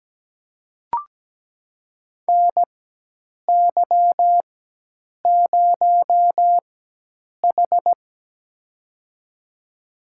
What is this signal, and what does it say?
Morse code, heard as a keyed tone, about 17 wpm, spaced as for 8 wpm, 700 Hz. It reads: NY0H
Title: USA Call Signs - 4 Characters - 4x Character Spacing - 17wpm